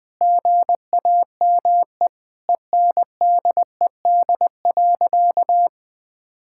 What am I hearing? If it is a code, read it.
Morse code, heard as a keyed tone, about 20 wpm, 700 Hz. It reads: GAME ENDED.
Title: Sentences with 2-Word Phrases - 20wpm